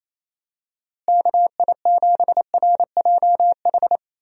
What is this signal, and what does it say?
Morse code, heard as a keyed tone, about 28 wpm, 700 Hz. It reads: KI7RJH
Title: USA Call Signs - 6 Characters - Repeated 3x - 28wpm